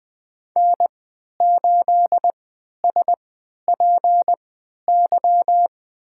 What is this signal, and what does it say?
Morse code, heard as a keyed tone, about 21 wpm, spaced as for 12 wpm, 700 Hz. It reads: N8SPY
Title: USA Call Signs - 5 Characters - 3x Character Spacing - 20wpm